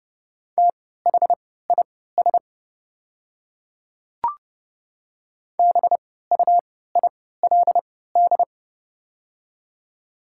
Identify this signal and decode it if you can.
Morse code, heard as a keyed tone, about 30 wpm, spaced as for 17 wpm, 700 Hz. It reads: THIS BUILD
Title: 3x Character Spacing - Top 200 Words - 30wpm